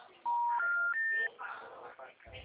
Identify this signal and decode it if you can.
Special information tone, with speech.